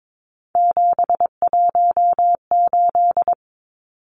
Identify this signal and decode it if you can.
Morse code, heard as a keyed tone, about 22 wpm, 700 Hz. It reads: 718
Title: Numbers - 3-Digit 22wpm